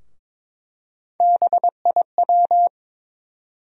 Morse code, heard as a keyed tone, about 22 wpm, 700 Hz. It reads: BIW
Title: USA Call Sign Postfixes - 3 Letters 22wpm